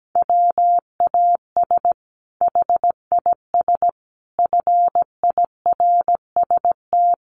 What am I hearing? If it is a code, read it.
Morse code, heard as a keyed tone, about 17 wpm, 700 Hz. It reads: WAS HIS FIRST
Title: Sentences from Top 800 Words 17wpm